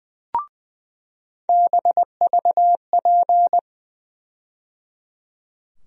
Morse code, heard as a keyed tone, about 20 wpm, 700 Hz. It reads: BVP
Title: USA Call Sign Postfixes - 3 Letters 20wpm